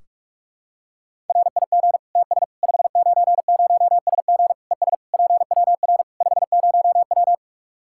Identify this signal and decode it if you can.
Morse code, heard as a keyed tone, about 46 wpm, 700 Hz. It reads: RIG TS 590SG ES PWR 50W
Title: K1USN SST QSO Practice 45wpm